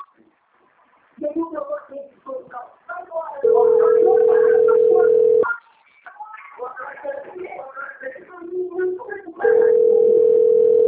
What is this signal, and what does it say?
Ringback tone (North America), 440 + 480 Hz, with speech.
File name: out- - - - - .55259